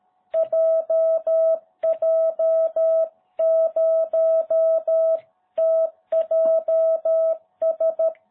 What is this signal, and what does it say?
Morse code, heard as a keyed tone, about 13 wpm, 660 Hz. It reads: JJ0TJS